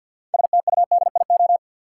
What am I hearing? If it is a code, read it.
Morse code, heard as a keyed tone, about 52 wpm, 700 Hz. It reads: STUDIO
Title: Sentences from Top 1200 Words - New Word Drill Alt - 50wpm